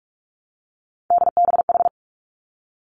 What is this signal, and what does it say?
Morse code, heard as a keyed tone, about 46 wpm, 700 Hz. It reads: DBH
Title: Three Challenging Letter-Number Combinations 45wpm